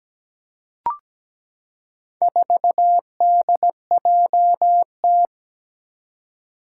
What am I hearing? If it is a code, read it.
Morse code, heard as a keyed tone, about 17 wpm, 700 Hz. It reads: 4DJT
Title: Four Challenging Letter-Number Combinations 17wpm